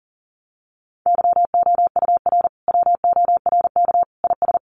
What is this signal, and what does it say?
Morse code, heard as a keyed tone, about 40 wpm, 700 Hz. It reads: YOUR WORK IS
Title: Sentences from Top 1100 Words 40wpm